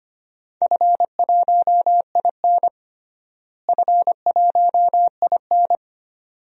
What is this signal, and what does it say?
Morse code, heard as a keyed tone, about 25 wpm, 700 Hz. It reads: F1IN F1IN
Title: DXCC-100 Call Signs - 4 Characters - Repeated 2x - 25wpm